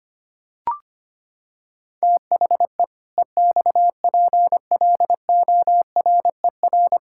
Morse code, heard as a keyed tone, about 25 wpm, 700 Hz. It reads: THE EXPLORER
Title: Pangram Challenge v2 - 25wpm